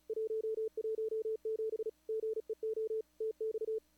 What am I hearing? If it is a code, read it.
Morse code, heard as a keyed tone, about 34 wpm, 410 Hz. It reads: 117 GEO TX
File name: CWSprint / Feb2022 / k5tr-feb-na-sprint-cw-k5ot -0000